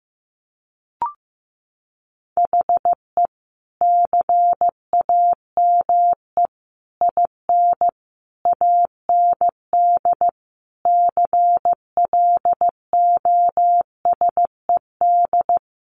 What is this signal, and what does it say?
Morse code, heard as a keyed tone, about 14 wpm, 700 Hz. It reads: HE CAME IN AND CLOSED
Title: Sentences from Top 1400 Words 15wpm